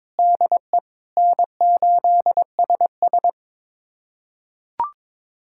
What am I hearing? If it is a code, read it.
Morse code, heard as a keyed tone, about 23 wpm, 700 Hz. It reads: DE N8SS
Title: USA Callsign DE Callsign - 4 Characters - 22wpm